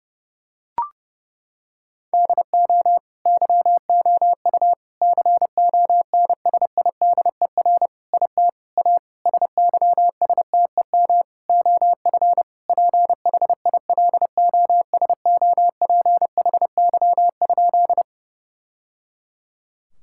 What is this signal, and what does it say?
Morse code, heard as a keyed tone, about 32 wpm, 700 Hz. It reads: DO YOU CONSIDER IT A SYSTEM OF PHILOSOPHY?
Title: Sentences from Top 1000 Words - Review Part 6 - 30wpm